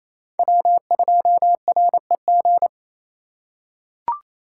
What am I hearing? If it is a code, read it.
Morse code, heard as a keyed tone, about 28 wpm, 700 Hz. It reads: W2REG
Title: USA Call Signs - 5 Characters - 2x Character Spacing - 28wpm